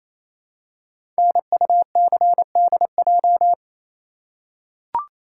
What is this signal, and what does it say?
Morse code, heard as a keyed tone, about 27 wpm, 700 Hz. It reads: NUCDJ
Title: Code Groups - 5 Letters - 28wpm